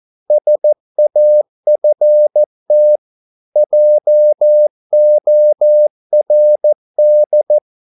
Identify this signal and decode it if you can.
Morse code, heard as a keyed tone, about 14 wpm, 600 Hz. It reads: SAFT JORD
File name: Korte ord DK 14wpm